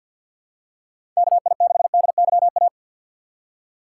Morse code, heard as a keyed tone, about 46 wpm, 700 Hz. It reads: KI6DYA
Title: USA Call Signs - 2x Character Spacing - 50wpm